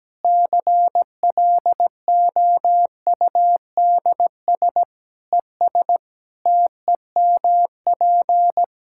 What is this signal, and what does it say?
Morse code, heard as a keyed tone, about 17 wpm, 700 Hz. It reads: CLOUDS ES TEMP
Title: Practice QSO - Exchange 2 - Info - 17wpm